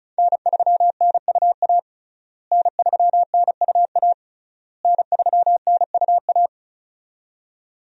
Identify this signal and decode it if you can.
Morse code, heard as a keyed tone, about 36 wpm, 700 Hz. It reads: N3NUA N3NUA N3NUA
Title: USA Call Signs - 5 Characters - Repeated 3x - 35wpm